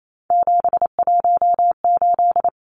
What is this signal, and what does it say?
Morse code, heard as a keyed tone, about 28 wpm, 700 Hz. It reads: 718
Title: Numbers - 3-Digit 28wpm